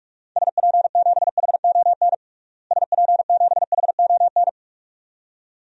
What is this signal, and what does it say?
Morse code, heard as a keyed tone, about 43 wpm, 700 Hz. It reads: SP7HON SP7HON
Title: DXCC-100 Call Signs - 6 Characters - Repeated 2x - 45wpm